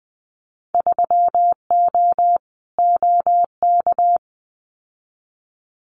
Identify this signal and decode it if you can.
Morse code, heard as a keyed tone, about 20 wpm, 700 Hz. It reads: 3O OK
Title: Winter Field Day Exchanges 20wpm